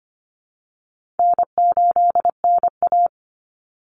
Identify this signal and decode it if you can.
Morse code, heard as a keyed tone, about 25 wpm, 700 Hz. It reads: N8NA
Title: Easy Call Signs Repeated 3x - 25wpm